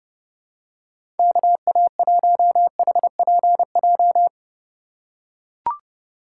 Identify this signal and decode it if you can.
Morse code, heard as a keyed tone, about 30 wpm, 700 Hz. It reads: KA1HPJ
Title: USA Call Signs - 6 Characters - 4x Character Spacing - 30wpm